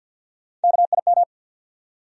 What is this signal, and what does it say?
Morse code, heard as a keyed tone, about 53 wpm, 700 Hz. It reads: KIM
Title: USA Common Names - Max 3 Letters - 50wpm